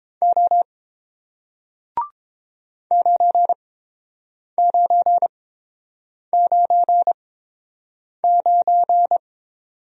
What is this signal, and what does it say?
Morse code, heard as a keyed tone, about 29 wpm, spaced as for 7 wpm, 700 Hz. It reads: O 9999